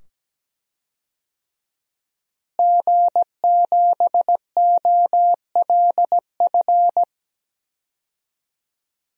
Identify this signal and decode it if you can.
Morse code, heard as a keyed tone, about 17 wpm, 700 Hz. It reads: G7OLF
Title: DXCC-100 Call Signs - 5 Characters - Repeated 2x - 17wpm